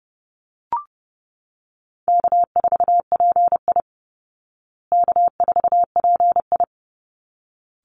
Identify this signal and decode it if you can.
Morse code, heard as a keyed tone, about 30 wpm, 700 Hz. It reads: K4PI K4PI
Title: Easy Call Signs Repeated 2x - 30wpm